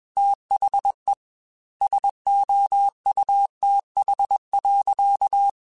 Morse code, heard as a keyed tone, about 21 wpm, 800 Hz. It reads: THE SOUTH.
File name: Weather Sets / Weather Sets-mp3s / Weather 4 10 @20